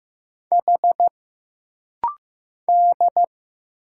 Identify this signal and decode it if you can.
Morse code, heard as a keyed tone, about 15 wpm, 700 Hz. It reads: H D
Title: Single Letters - Rapid-Fire 15wpm